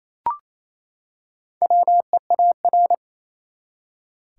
Morse code, heard as a keyed tone, about 28 wpm, 700 Hz. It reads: WEAR
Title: Four Letter Words - 28wpm